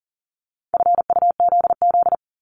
Morse code, heard as a keyed tone, about 38 wpm, 700 Hz. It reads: FUZZ
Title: Multi-letter Phonemes 40wpm